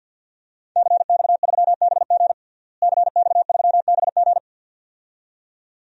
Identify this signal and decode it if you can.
Morse code, heard as a keyed tone, about 52 wpm, 700 Hz. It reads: CX3BG CX3BG
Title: DXCC-100 Call Signs - 5 Characters - Repeated 2x - 50wpm